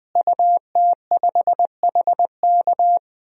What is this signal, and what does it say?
Morse code, heard as a keyed tone, about 20 wpm, 700 Hz. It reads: UT5HK
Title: DXCC-100 Call Signs - 5 Characters - Repeated 2x - 20wpm